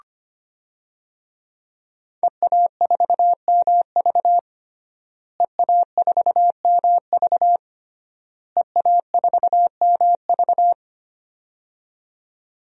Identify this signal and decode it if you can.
Morse code, heard as a keyed tone, about 25 wpm, 700 Hz. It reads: EA4MV EA4MV EA4MV
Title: DXCC-100 Call Signs - 5 Characters - Repeated 3x - 25wpm